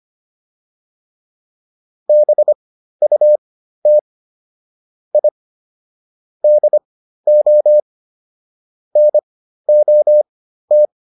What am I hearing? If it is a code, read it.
Morse code, heard as a keyed tone, about 25 wpm, spaced as for 13 wpm, 600 Hz. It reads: BUT I DO NOT